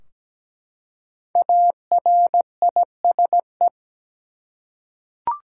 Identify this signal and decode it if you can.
Morse code, heard as a keyed tone, about 17 wpm, 700 Hz. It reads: ARISE
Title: Five Letter Words - 17wpm